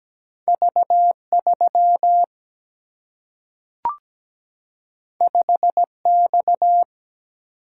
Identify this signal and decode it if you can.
Morse code, heard as a keyed tone, about 17 wpm, 700 Hz. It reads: V 3 5 X
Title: Single-Letter Single-Number 17wpm